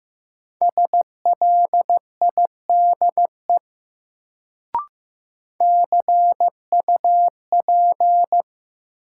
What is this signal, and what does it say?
Morse code, heard as a keyed tone, about 15 wpm, 700 Hz. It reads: SLIDE CUP
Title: Sentences from Top 1600 Words - New Word Drill Alt - 15wpm